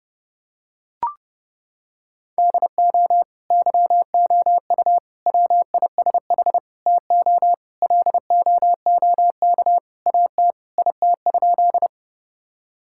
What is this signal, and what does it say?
Morse code, heard as a keyed tone, about 32 wpm, 700 Hz. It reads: DO YOU WISH TO LOOK AT IT?
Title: Sentences from Top 1600 Words 30wpm